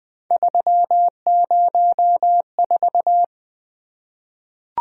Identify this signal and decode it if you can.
Morse code, heard as a keyed tone, about 20 wpm, 700 Hz. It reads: 304
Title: Three Letter-Number Combinations - 20wpm